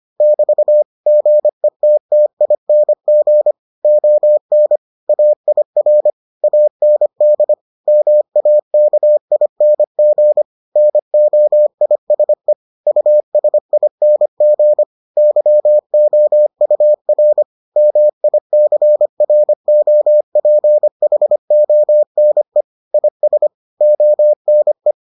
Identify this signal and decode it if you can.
Morse code, heard as a keyed tone, about 25 wpm, 600 Hz. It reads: <BT> GETTING ON AIR AND MAKING NOISE USING YOUR MICROPHONE IS ONE